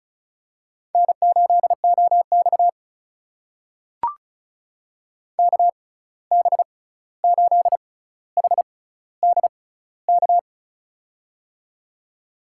Morse code, heard as a keyed tone, about 36 wpm, spaced as for 16 wpm, 700 Hz. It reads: N8OX KB8HDK
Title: USA Call Signs - 6x Character Spacing - 35wpm